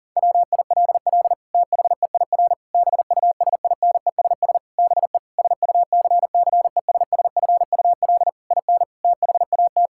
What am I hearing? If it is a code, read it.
Morse code, heard as a keyed tone, about 37 wpm, 700 Hz. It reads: WILL THEIR BUSINESS BE SUCCESSFUL IN THAT
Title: Sentences from Top 1000 Words - Review Part 5 - 40wpm